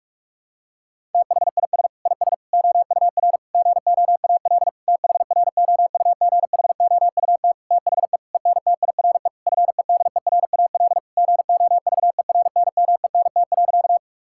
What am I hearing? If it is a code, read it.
Morse code, heard as a keyed tone, about 46 wpm, 700 Hz. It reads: THIS IS OUR GOAL THROUGHOUT THE ENTIRE FEDERAL GOVERNMENT.
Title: Sentences from Top 1000 Words - Review Part 5 - 45wpm